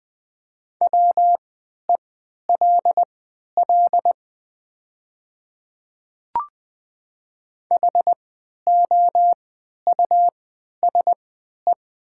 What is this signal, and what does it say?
Morse code, heard as a keyed tone, about 20 wpm, spaced as for 11 wpm, 700 Hz. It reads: WELL HOUSE
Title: 3x Character Spacing - Top 300 Words - 20wpm